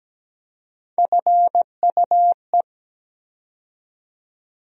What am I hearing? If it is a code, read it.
Morse code, heard as a keyed tone, about 17 wpm, 700 Hz. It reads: FUE
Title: USA Call Sign Postfixes - 3 Letters 17wpm